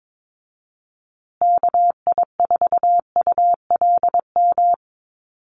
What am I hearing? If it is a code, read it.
Morse code, heard as a keyed tone, about 22 wpm, 700 Hz. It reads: KI4ULM
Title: USA Call Signs Repeated 3x 22wpm